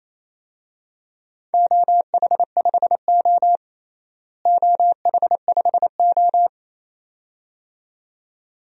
Morse code, heard as a keyed tone, about 28 wpm, 700 Hz. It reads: OH5O OH5O
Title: DXCC-100 Call Signs - 4 Characters - Repeated 2x - 28wpm